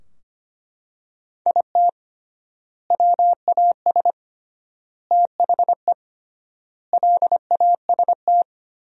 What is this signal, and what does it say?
Morse code, heard as a keyed tone, about 25 wpm, 700 Hz. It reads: IT WAS THE LAST
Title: 3x Word-Spacing - Sentences from Top 300 Words - 25wpm